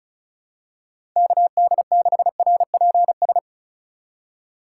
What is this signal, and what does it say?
Morse code, heard as a keyed tone, about 34 wpm, 700 Hz. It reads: KD6RPS
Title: USA Call Signs - 4x Character Spacing - 35wpm